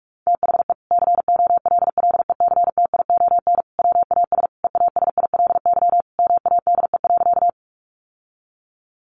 Morse code, heard as a keyed tone, about 44 wpm, 700 Hz. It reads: THE COLLECTION WAS EASILY MADE.
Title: Sentences from Top 1200 Words 45wpm